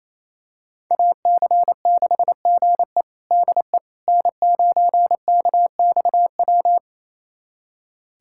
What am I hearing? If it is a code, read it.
Morse code, heard as a keyed tone, about 27 wpm, 700 Hz. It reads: AC6GE DE N9KXW
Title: USA Callsign DE Callsign - 5 Characters - 28wpm